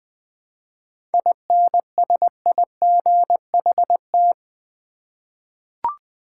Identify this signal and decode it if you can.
Morse code, heard as a keyed tone, about 20 wpm, 690 Hz. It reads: INSIGHT
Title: 7 Letter Words - First 3 Letters Given - 20wpm